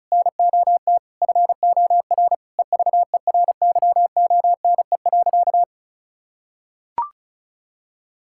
Morse code, heard as a keyed tone, about 34 wpm, 700 Hz. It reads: NOT FOR EVERYONE.
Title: Sentences from Top 1000 Words - Review Part 5 - 35wpm